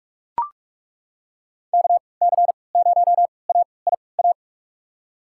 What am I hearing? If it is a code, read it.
Morse code, heard as a keyed tone, about 42 wpm, spaced as for 25 wpm, 700 Hz. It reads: KC0AIA
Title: USA Call Signs - 6 Characters - 3x Character Spacing - 45wpm